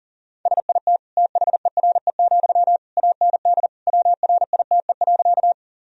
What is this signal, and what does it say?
Morse code, heard as a keyed tone, about 40 wpm, 700 Hz. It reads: SIT THERE, AND WRITE.
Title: Sentences from Top 1000 Words - Review Part 1 - 40wpm